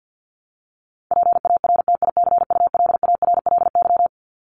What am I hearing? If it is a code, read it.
Morse code, heard as a keyed tone, about 50 wpm, 700 Hz. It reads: PARTICULARLY
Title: LY Suffix - Top 100 Words with Base Words 50wpm